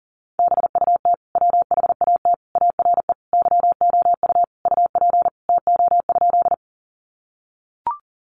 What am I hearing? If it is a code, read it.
Morse code, heard as a keyed tone, about 40 wpm, 700 Hz. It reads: BUT WHAT ARE YOU UP TO?
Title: Sentences from Top 100 Words 40wpm